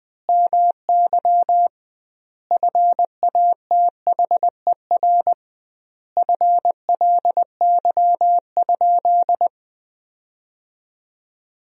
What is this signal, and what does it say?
Morse code, heard as a keyed tone, about 20 wpm, 700 Hz. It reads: MY FATHER FLY?
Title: 2x Word-Spacing - Sentences from Top 500 Words - 20wpm